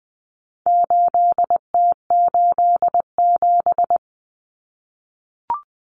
Morse code, heard as a keyed tone, about 20 wpm, 700 Hz. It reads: 8T87
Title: Common Cut Numbers - 4-Digit 20wpm